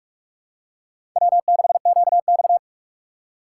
Morse code, heard as a keyed tone, about 42 wpm, 700 Hz. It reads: W6QX
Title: USA Call Signs - 4 Characters - 8x Character Spacing - 45wpm